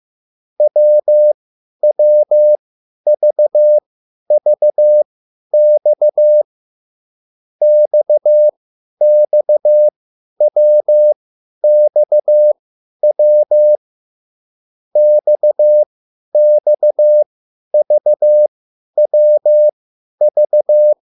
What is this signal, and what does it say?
Morse code, heard as a keyed tone, about 15 wpm, spaced as for 10 wpm, 600 Hz. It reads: WWVVX XXWXW XXVWV